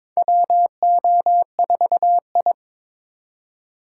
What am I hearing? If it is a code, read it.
Morse code, heard as a keyed tone, about 22 wpm, 700 Hz. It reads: WO4I
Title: USA Call Signs - 2x Character Spacing - 22wpm